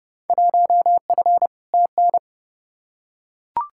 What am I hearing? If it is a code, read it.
Morse code, heard as a keyed tone, about 31 wpm, 700 Hz. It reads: 1F TN
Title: Field Day Exchanges - 30wpm